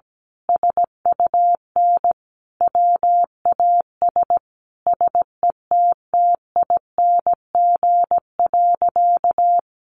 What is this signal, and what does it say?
Morse code, heard as a keyed tone, about 17 wpm, 700 Hz. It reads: SUN WAS SETTING.
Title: Sentences from Top 1300 Words 17wpm